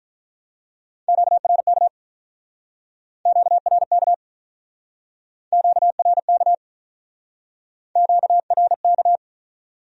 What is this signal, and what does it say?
Morse code, heard as a keyed tone, about 47 wpm, 700 Hz. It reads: QRK QRK QRK QRK